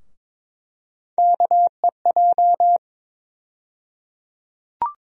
Morse code, heard as a keyed tone, about 22 wpm, 700 Hz. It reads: KEJ
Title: USA Call Sign Postfixes - 3 Letters 22wpm